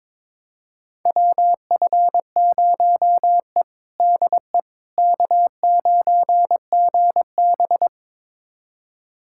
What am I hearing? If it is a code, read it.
Morse code, heard as a keyed tone, about 22 wpm, 700 Hz. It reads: WF0E DE K9GB
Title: USA Callsign DE Callsign - 4 Characters - 22wpm